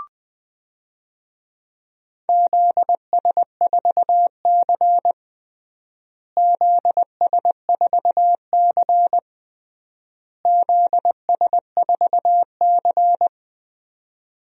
Morse code, heard as a keyed tone, about 20 wpm, 700 Hz. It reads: ZS4C ZS4C ZS4C
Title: DXCC-100 Call Signs - 4 Characters - Repeated 3x - 20wpm